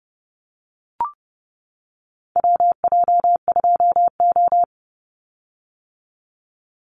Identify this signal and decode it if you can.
Morse code, heard as a keyed tone, about 30 wpm, 700 Hz. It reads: WJ2O
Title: Call Signs - 30wpm